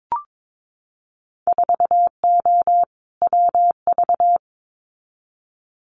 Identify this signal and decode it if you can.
Morse code, heard as a keyed tone, about 22 wpm, 700 Hz. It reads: 4O WV
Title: Winter Field Day Exchanges Encore 22wpm